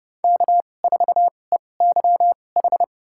Morse code, heard as a keyed tone, about 31 wpm, spaced as for 22 wpm, 700 Hz. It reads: K4EYH
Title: USA Call Signs - 2x Character Spacing - 30wpm